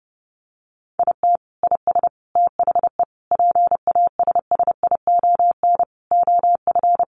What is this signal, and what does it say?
Morse code, heard as a keyed tone, about 30 wpm, 700 Hz. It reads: IT IS THE PASSION OF
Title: Sentences from Top 1300 Words 30wpm